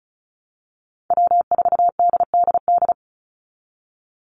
Morse code, heard as a keyed tone, about 34 wpm, 700 Hz. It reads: W4DDD
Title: Call Signs Repeated 2x - 35wpm